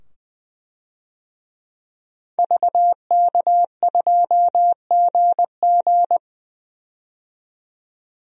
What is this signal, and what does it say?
Morse code, heard as a keyed tone, about 20 wpm, 700 Hz. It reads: VK2GG
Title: DXCC-100 Call Signs - 5 Characters - 20wpm